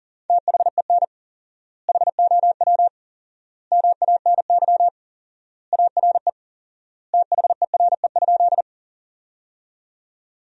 Morse code, heard as a keyed tone, about 40 wpm, 700 Hz. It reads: THEN HOW MANY ARE THERE?
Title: 4x Word-Spacing - Sentences from Top 100 Words - 40wpm